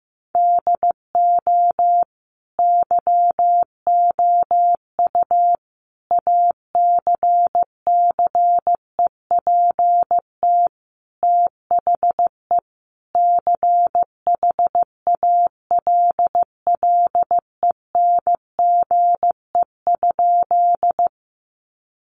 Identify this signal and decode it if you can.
Morse code, heard as a keyed tone, about 15 wpm, 700 Hz. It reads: DO YOU ACCEPT THE CHALLENGE?
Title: Sentences from Top 1300 Words 15wpm